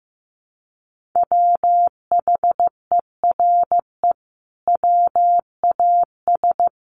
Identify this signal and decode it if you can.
Morse code, heard as a keyed tone, about 15 wpm, 700 Hz. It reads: WHERE WAS
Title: Sentences from Top 1400 Words 15wpm